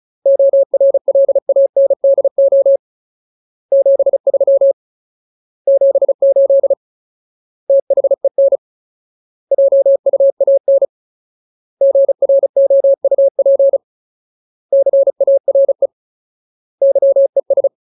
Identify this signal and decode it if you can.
Morse code, heard as a keyed tone, about 34 wpm, 550 Hz. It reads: ORLANDO 73 78 THEN JUAN GROUP CARE YES
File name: Ultimate Word List / 35WPM Ultimate